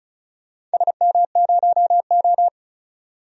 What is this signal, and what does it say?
Morse code, heard as a keyed tone, about 34 wpm, 700 Hz. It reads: SM0O
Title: DXCC-100 Call Signs - 4 Characters - 35wpm